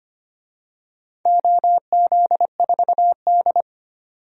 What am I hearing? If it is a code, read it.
Morse code, heard as a keyed tone, about 25 wpm, 700 Hz. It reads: OZ4D
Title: DXCC-100 Call Signs - 4 Characters - 25wpm